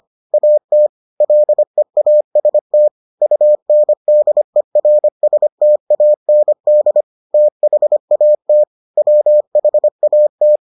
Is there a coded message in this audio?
Yes: Morse code from 0.3 to 10.6 s, heard as a keyed tone, about 25 wpm, 600 Hz. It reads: AT LEAST UNDERSTAND THAT WHAT